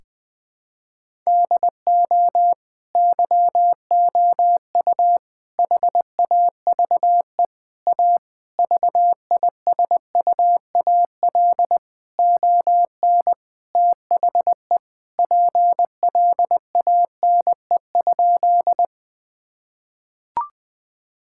Morse code, heard as a keyed tone, about 20 wpm, 700 Hz. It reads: DO YOU HAVE A VISUAL ON THE PLANE?
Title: Sentences from Top 1000 Words - Review Part 6 - 20wpm